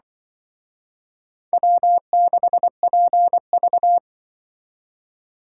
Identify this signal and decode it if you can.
Morse code, heard as a keyed tone, about 24 wpm, 700 Hz. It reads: W6PV
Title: Speed Racing - USA 1x2 Call Signs 20wpm